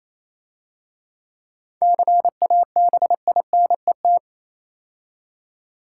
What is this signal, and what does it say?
Morse code, heard as a keyed tone, about 28 wpm, 700 Hz. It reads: CABINET